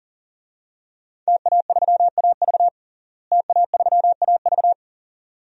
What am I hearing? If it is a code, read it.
Morse code, heard as a keyed tone, about 40 wpm, 700 Hz. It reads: TA3AV TA3AV
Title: DXCC-100 Call Signs - 5 Characters - Repeated 2x - 40wpm